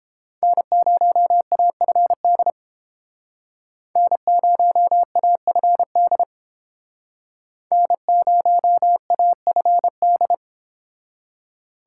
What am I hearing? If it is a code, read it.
Morse code, heard as a keyed tone, about 29 wpm, 700 Hz. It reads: N0AFD N0AFD N0AFD